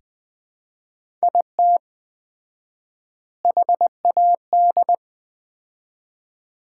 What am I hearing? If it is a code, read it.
Morse code, heard as a keyed tone, about 21 wpm, 700 Hz. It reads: IT HAD
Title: 4x Word-Spacing - Sets of 3 Words - 20wpm